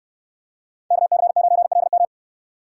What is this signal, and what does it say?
Morse code, heard as a keyed tone, about 66 wpm, 700 Hz. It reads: KC9LM